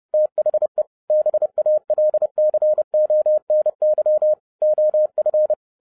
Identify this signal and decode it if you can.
Morse code, heard as a keyed tone, about 30 wpm, 610 Hz. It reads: THE BALCONY OF